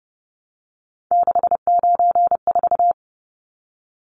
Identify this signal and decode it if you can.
Morse code, heard as a keyed tone, about 30 wpm, 700 Hz. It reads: 694
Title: Numbers - 3-Digit 30wpm